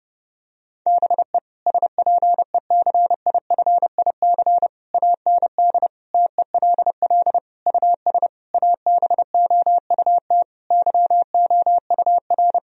Morse code, heard as a keyed tone, about 32 wpm, 700 Hz. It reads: BE SPECIFIC AND TELL US ABOUT YOUR
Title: Sentences from Top 1000 Words - Review Part 3 - 30wpm